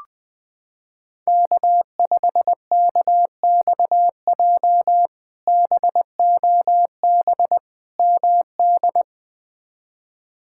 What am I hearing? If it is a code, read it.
Morse code, heard as a keyed tone, about 20 wpm, 700 Hz. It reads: K5KXJ BOB MD
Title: K1USN SST QSO Practice - 20wpm